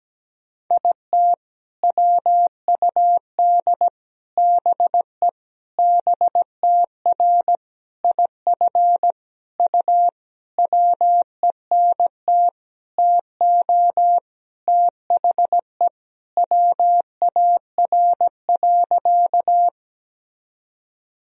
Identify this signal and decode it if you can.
Morse code, heard as a keyed tone, about 17 wpm, 700 Hz. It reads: IT WUD BE BTR IF U WENT TO THE WAR.
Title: Sentences from Top 400 Words and Abbreviations 17wpm